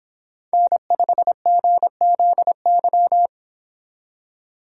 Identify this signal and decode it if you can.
Morse code, heard as a keyed tone, about 25 wpm, 700 Hz. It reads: N5GZY